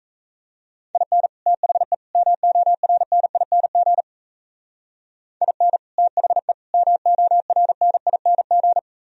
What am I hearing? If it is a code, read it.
Morse code, heard as a keyed tone, about 33 wpm, 700 Hz. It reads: IN THE MORNING IN THE MORNING
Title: Speed-Racing - Sets of 3 Words 28wpm